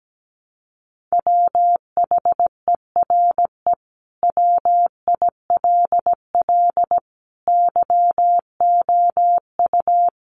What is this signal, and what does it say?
Morse code, heard as a keyed tone, about 17 wpm, 700 Hz. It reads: WHERE WILL YOU
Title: Sentences from Top 1300 Words 17wpm